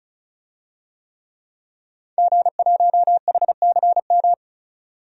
Morse code, heard as a keyed tone, about 36 wpm, 700 Hz. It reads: G1HCM
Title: DXCC-100 Call Signs - 5 Characters - 35wpm